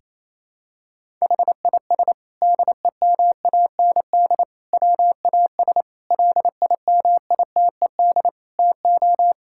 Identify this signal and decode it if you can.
Morse code, heard as a keyed tone, about 27 wpm, 700 Hz. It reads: HIS DEMAND WAS LIMITED TO
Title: Sentences from Top 1000 Words - Review Part 6 - 28wpm